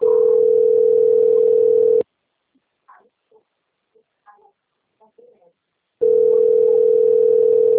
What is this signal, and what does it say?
Ringback tone (North America), 440 + 480 Hz, with speech.
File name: rg-1600-unknown- - - .60045